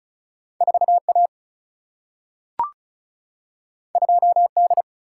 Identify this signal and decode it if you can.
Morse code, heard as a keyed tone, about 36 wpm, 700 Hz. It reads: 4 A 2 D
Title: Single-Letter Single-Number 35wpm